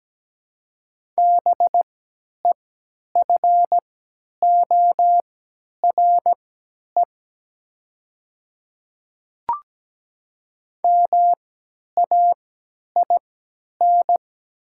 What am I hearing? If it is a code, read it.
Morse code, heard as a keyed tone, about 17 wpm, spaced as for 10 wpm, 700 Hz. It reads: BEFORE MAIN
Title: 3x Character Spacing - Top 300 Words - 17wpm